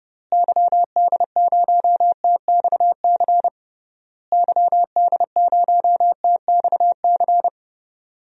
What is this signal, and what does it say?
Morse code, heard as a keyed tone, about 30 wpm, 700 Hz. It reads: YD0TXC YD0TXC
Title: DXCC-100 Call Signs - 6 Characters - Repeated 3x - 30wpm